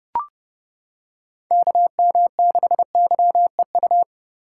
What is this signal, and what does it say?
Morse code, heard as a keyed tone, about 30 wpm, 700 Hz. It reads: KM6YEU